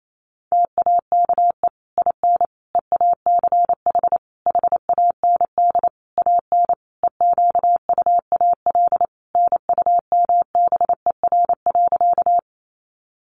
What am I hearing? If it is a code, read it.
Morse code, heard as a keyed tone, about 28 wpm, 700 Hz. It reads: TAKE IN EACH HAND AN EQUAL NUMBER.
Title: Sentences from Top 1400 Words 28wpm